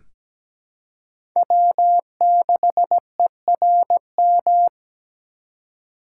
Morse code, heard as a keyed tone, about 17 wpm, 700 Hz. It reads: W6ERM
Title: USA Call Signs - 5 Characters - 4x Character Spacing - 17wpm